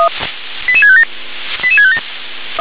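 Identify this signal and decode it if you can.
Five-tone selective call (ZVEI), heard as quick runs of five tones, tones of about 70 ms. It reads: 88657 88657
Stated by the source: ZVEI five-tone selective call 88651 received on SDR.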